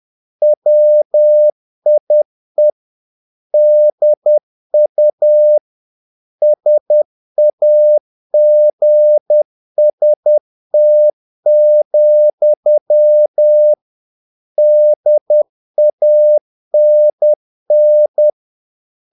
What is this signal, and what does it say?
Morse code, heard as a keyed tone, about 10 wpm, 600 Hz. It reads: WIE DU SAGST, DANN
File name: dk5ke / Humor-50BpM